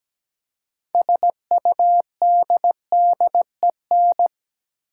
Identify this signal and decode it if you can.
Morse code, heard as a keyed tone, about 17 wpm, 700 Hz. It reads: SUDDEN
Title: Six Letter Words - 17wpm